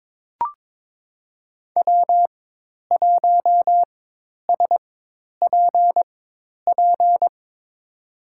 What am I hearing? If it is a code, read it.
Morse code, heard as a keyed tone, about 22 wpm, spaced as for 10 wpm, 700 Hz. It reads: W1SPP
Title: USA Call Signs - 5 Characters - 4x Character Spacing - 22wpm